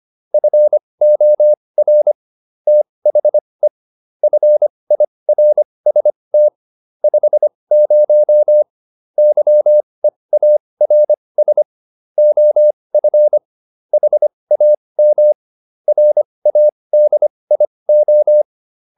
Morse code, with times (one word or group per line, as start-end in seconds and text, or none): 0.3-2.1 FOR
2.7-3.7 THE
4.2-6.5 FIRST
7.0-8.6 50
9.2-11.6 YEARS
12.2-13.4 OF
13.9-15.3 HAM
15.9-18.4 RADIO